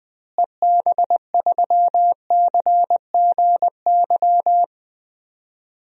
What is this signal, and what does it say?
Morse code, heard as a keyed tone, about 20 wpm, 700 Hz. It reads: EB3CGY
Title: DXCC-100 Call Signs - 6 Characters - Repeated 3x - 20wpm